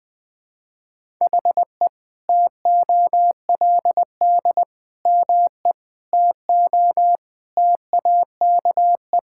Morse code, heard as a keyed tone, about 20 wpm, 700 Hz. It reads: HE TOLD ME TO TAKE
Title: Sentences from Top 1000 Words - Review Part 6 - 20wpm